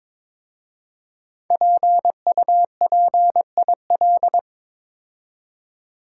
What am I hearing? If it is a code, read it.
Morse code, heard as a keyed tone, about 22 wpm, 700 Hz. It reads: PUPIL
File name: 5 Letter Words - First 3 Letters Given - 22wpm